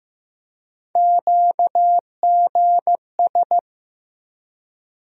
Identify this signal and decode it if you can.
Morse code, heard as a keyed tone, about 15 wpm, 700 Hz. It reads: QGS
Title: USA Call Sign Postfixes - 3 Letters 15wpm